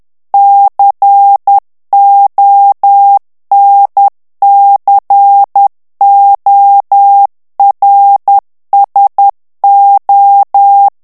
Morse code, heard as a keyed tone, about 10 wpm, 800 Hz. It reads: CONCORSO
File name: ITALIEN texte clair 2 vitesse 10 mots minute